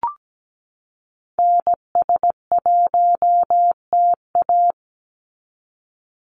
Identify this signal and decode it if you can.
Morse code, heard as a keyed tone, about 17 wpm, 700 Hz. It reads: NS1TA
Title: NA SOTA Activator Callsigns 17wpm